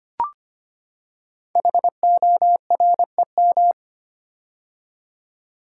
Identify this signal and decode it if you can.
Morse code, heard as a keyed tone, about 25 wpm, 700 Hz. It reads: HOREM
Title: Code Groups - 5 Letters - 25wpm